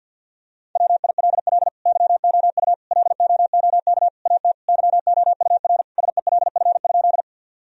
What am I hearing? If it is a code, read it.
Morse code, heard as a keyed tone, about 49 wpm, 700 Hz. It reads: WILL YOU LOOK AT YOUR SELF?
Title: Sentences from Top 1000 Words - Review Part 1 - 50wpm